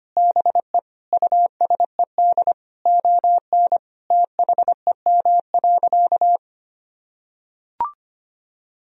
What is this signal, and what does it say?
Morse code, heard as a keyed tone, about 25 wpm, 700 Hz. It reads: BE USED ON THEM.
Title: Sentences from Top 1000 Words - Review Part 3 - 25wpm